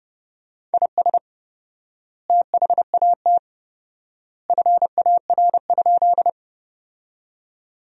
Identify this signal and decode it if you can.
Morse code, heard as a keyed tone, about 30 wpm, 700 Hz. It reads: IS THAT FAR?
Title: 4x Word-Spacing - Sentences from Top 300 Words - 30wpm